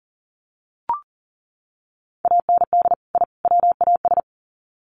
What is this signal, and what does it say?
Morse code, heard as a keyed tone, about 40 wpm, 700 Hz. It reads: AND I WAS
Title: Sets of 3 Words 40wpm